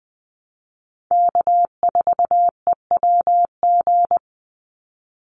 Morse code, heard as a keyed tone, about 20 wpm, 700 Hz. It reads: K4EWG
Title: Call Signs Repeated 2x - 20wpm